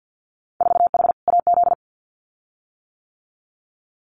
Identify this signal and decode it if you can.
Morse code, heard as a keyed tone, about 50 wpm, 700 Hz. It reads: 4H AZ
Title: Winter Field Day Exchanges Encore 50wpm